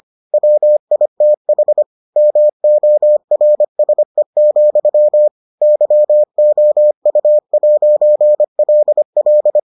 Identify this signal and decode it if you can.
Morse code, heard as a keyed tone, about 25 wpm, 600 Hz. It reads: WITH MORSE, YOU'LL